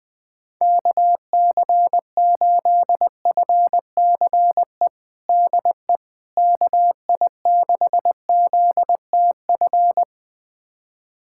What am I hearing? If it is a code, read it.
Morse code, heard as a keyed tone, about 20 wpm, 700 Hz. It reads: KC8FCE DE KI6ZTF
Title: USA Callsign DE Callsign - 6 Characters - 20wpm